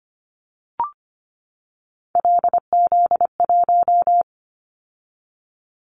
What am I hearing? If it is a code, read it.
Morse code, heard as a keyed tone, about 25 wpm, 700 Hz. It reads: LZ1
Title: Callsign Prefixes for Top 25 DXCC - 25wpm